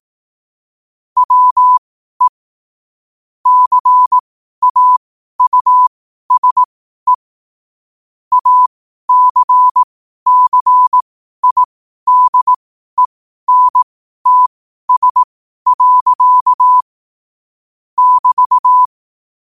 Morse code, heard as a keyed tone, about 18 wpm, spaced as for 12 wpm, 1000 Hz. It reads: WE CAUSE ACCIDENTS. <BT>
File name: qotdfiles / 12wpm